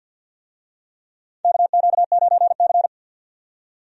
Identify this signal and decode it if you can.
Morse code, heard as a keyed tone, about 50 wpm, 700 Hz. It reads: KQ9C
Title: USA Call Signs - 4 Characters - 50wpm